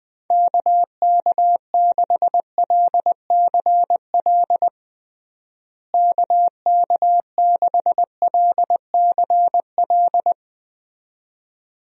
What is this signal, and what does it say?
Morse code, heard as a keyed tone, about 21 wpm, 700 Hz. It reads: KK6LCL KK6LCL
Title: USA Call Signs - 6 Characters - Repeated 2x - 20wpm